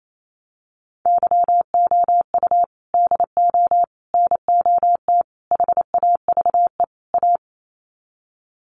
Morse code, heard as a keyed tone, about 28 wpm, 700 Hz. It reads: YOU DO NOT HAVE A
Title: Sets of 5 Words 28wpm